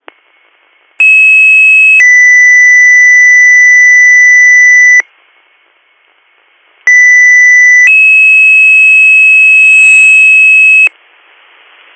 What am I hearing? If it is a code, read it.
Two-tone page, heard as a one-second tone, then a three-second tone, about 2600 then 2000 Hz, about 2000 then 2600 Hz. It is heard from 1.0 to 10.9 s.